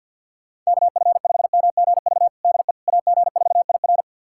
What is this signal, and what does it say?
Morse code, heard as a keyed tone, about 51 wpm, 690 Hz. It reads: KF5MGV DE AG4IR
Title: USA Callsign DE Callsign 50wpm